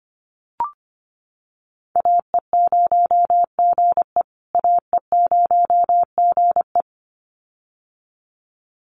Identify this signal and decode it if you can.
Morse code, heard as a keyed tone, about 25 wpm, 700 Hz. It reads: AE0GE AE0GE
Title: USA Call Signs Repeated 2x 25wpm